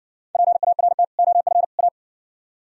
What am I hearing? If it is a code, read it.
Morse code, heard as a keyed tone, about 59 wpm, 700 Hz. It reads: PART OF A